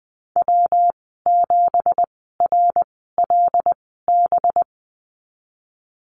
Morse code, heard as a keyed tone, about 20 wpm, spaced as for 15 wpm, 700 Hz. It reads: W7RLB
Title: 186 Lesson 30g - Letter b - Call Signs 20wpm